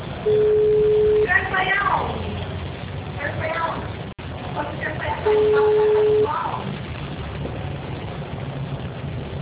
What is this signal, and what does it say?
Ringback tone (Europe, 425 Hz), with speech.
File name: out- -2014- - - .9434